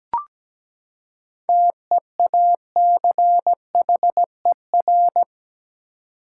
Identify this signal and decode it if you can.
Morse code, heard as a keyed tone, about 17 wpm, 700 Hz. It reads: TEACHER
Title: Seven Letter Words - 17wpm